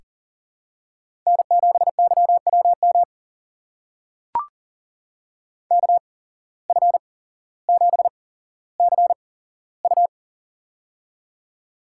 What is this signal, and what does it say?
Morse code, heard as a keyed tone, about 42 wpm, spaced as for 17 wpm, 700 Hz. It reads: N7YWM KF7CU
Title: USA Call Signs - 5 Characters - 8x Character Spacing - 40wpm